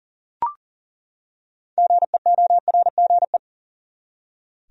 Morse code, heard as a keyed tone, about 40 wpm, 700 Hz. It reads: GEORGE
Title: USA Common Names - 40wpm